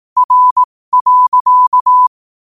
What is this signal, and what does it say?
Morse code, heard as a keyed tone, about 18 wpm, spaced as for 15 wpm, 1000 Hz. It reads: R.
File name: qotdfiles / 15wpm